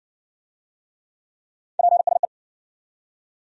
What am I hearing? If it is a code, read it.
Morse code, heard as a keyed tone, about 52 wpm, 700 Hz. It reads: PSE